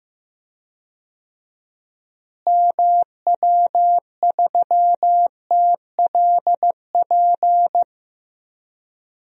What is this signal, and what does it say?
Morse code, heard as a keyed tone, about 16 wpm, 700 Hz. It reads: MW3TLP
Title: DXCC-100 Call Signs - 6 Characters - Repeated 2x - 15wpm